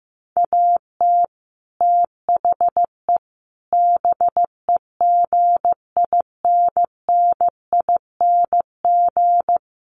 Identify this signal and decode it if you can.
Morse code, heard as a keyed tone, about 15 wpm, 700 Hz. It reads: AT THE BEGINNING
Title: Sets of 5 Words 15wpm